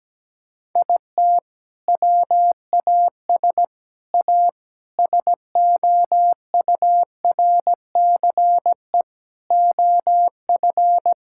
Morse code, heard as a keyed tone, about 17 wpm, 700 Hz. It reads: IT WAS A SOURCE OF
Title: Sentences from Top 1000 Words - Review Part 5 - 17wpm